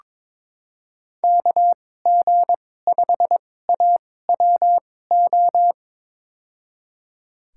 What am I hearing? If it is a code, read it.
Morse code, heard as a keyed tone, about 22 wpm, spaced as for 16 wpm, 700 Hz. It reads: KG5AWO
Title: USA Call Signs - 6 Characters - 2x Character Spacing - 22wpm